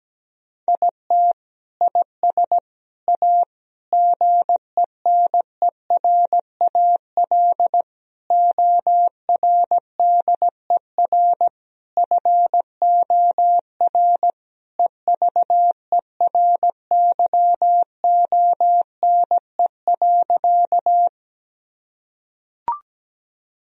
Morse code, heard as a keyed tone, about 17 wpm, 700 Hz. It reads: IT IS A GENERAL ORDER FOR EVERYONE.
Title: Sentences from Top 1000 Words - Review Part 3 - 17wpm